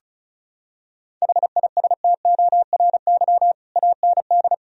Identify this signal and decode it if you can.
Morse code, heard as a keyed tone, about 34 wpm, 700 Hz. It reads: HISTORY AND
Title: Sentences from Top 1000 Words - Review Part 2 - 35wpm